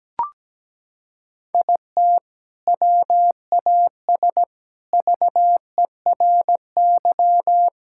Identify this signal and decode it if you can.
Morse code, heard as a keyed tone, about 17 wpm, 700 Hz. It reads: IT WAS VERY
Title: Sentences from Top 1000 Words - Review Part 6 - 17wpm